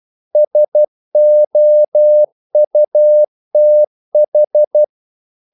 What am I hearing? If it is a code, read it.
Morse code, heard as a keyed tone, about 12 wpm, 600 Hz. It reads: SOUTH
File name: CW / Practice / NationalCapitals12